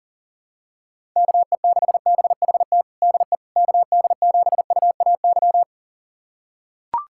Morse code, heard as a keyed tone, about 40 wpm, 700 Hz. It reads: KE6BHT DE KD7UAY
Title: USA Callsign DE Callsign 40wpm